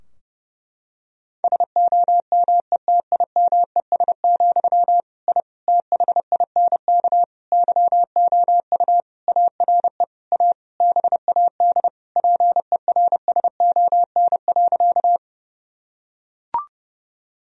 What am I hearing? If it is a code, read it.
Morse code, heard as a keyed tone, about 32 wpm, 700 Hz. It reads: SOMETIMES, I THINK YOU ARE A BAD PERSON.
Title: Sentences from Top 1000 Words - Review Part 3 - 30wpm